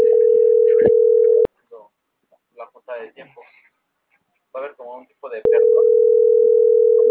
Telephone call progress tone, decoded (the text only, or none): ringback tone